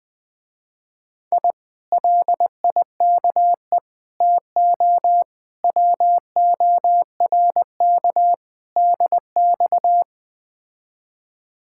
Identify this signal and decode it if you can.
Morse code, heard as a keyed tone, about 20 wpm, 700 Hz. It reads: I LIKE TO WORK DX
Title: Speed-Racing Common QSO Phrases Encore - 17wpm